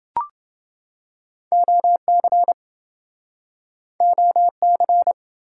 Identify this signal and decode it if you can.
Morse code, heard as a keyed tone, about 29 wpm, 700 Hz. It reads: OC OC